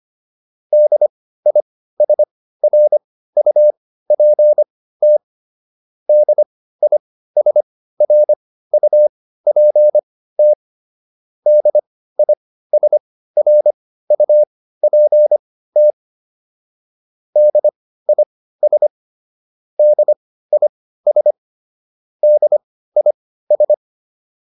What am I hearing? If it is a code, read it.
Morse code, heard as a keyed tone, about 25 wpm, spaced as for 15 wpm, 600 Hz. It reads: DISRUPT DISRUPT DISRUPT DIS DIS DIS